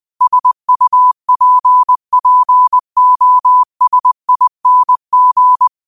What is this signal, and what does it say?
Morse code, heard as a keyed tone, about 20 wpm, 1000 Hz. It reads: SUPPOSING
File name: qotdfiles / 20wpm